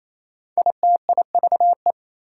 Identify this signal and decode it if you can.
Morse code, heard as a keyed tone, about 28 wpm, 700 Hz. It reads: ITIVE
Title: Prefixes and Suffixes 28wpm